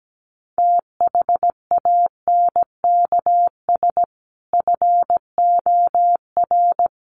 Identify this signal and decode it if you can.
Morse code, heard as a keyed tone, about 17 wpm, 700 Hz. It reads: THANKS FOR
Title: Sets of 5 Words 17wpm